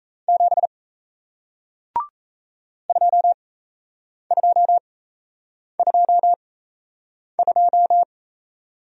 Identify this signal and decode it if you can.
Morse code, heard as a keyed tone, about 38 wpm, spaced as for 8 wpm, 700 Hz. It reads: 7 2222